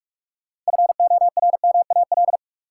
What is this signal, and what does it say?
Morse code, heard as a keyed tone, about 45 wpm, 700 Hz. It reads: FORMAL
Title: Sentences from Top 1300 Words - New Word Drill - 45wpm